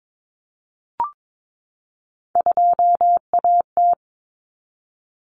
Morse code, heard as a keyed tone, about 22 wpm, 700 Hz. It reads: 2AT
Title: Common Cut Numbers - 3-Digit 22wpm